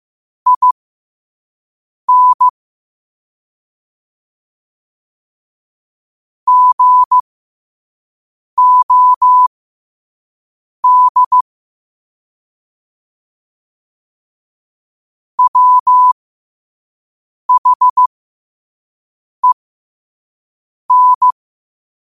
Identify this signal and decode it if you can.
Morse code, heard as a keyed tone, about 15 wpm, spaced as for 5 wpm, 1000 Hz. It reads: IN GOD WHEN
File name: qotdfiles / 5wpm